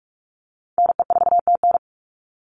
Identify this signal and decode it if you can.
Morse code, heard as a keyed tone, about 44 wpm, 700 Hz. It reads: NE4TN
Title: NA SOTA Chaser Callsigns 45wpm